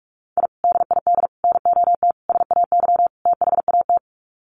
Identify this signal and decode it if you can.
Morse code, heard as a keyed tone, about 44 wpm, 700 Hz. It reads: I DID NOT SAY THAT
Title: Sets of 5 Words Encore2 45wpm